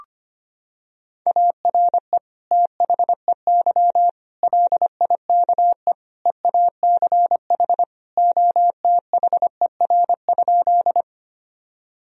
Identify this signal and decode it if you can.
Morse code, heard as a keyed tone, about 25 wpm, 700 Hz. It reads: ARE THEY LIKE EACH OTHER?
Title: Sentences from Top 1000 Words - Review Part 1 - 25wpm